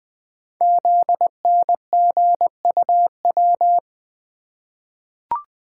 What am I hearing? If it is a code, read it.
Morse code, heard as a keyed tone, about 20 wpm, 700 Hz. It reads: ZNGUW
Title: Code Groups - 5 Letters - 20wpm